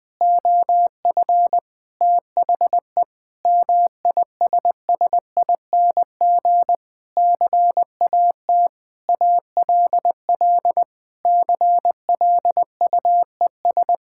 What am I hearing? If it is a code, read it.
Morse code, heard as a keyed tone, about 20 wpm, 700 Hz. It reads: OF THE MISSING CAT ALL CLUES
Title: Sentences with 4-Word Phrases - 20wpm